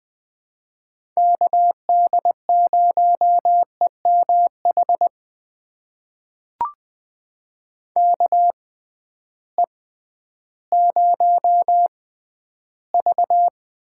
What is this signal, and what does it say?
Morse code, heard as a keyed tone, about 20 wpm, spaced as for 12 wpm, 700 Hz. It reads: KD0EMH KE0V
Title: USA Call Signs - 6 Characters - 6x Character Spacing - 20wpm